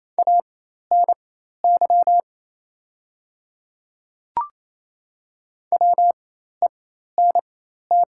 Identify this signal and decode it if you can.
Morse code, heard as a keyed tone, about 27 wpm, spaced as for 13 wpm, 700 Hz. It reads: ANY WENT
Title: 4x Character Spacing - Top 300 Words - 28wpm